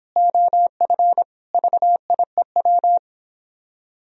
Morse code, heard as a keyed tone, about 26 wpm, 700 Hz. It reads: OF VIEW